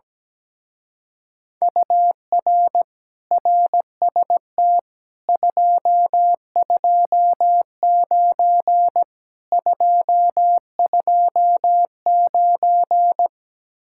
Morse code, heard as a keyed tone, about 17 wpm, 700 Hz. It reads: UR RST 229 229
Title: Speed-Racing Common QSO Phrases - 17wpm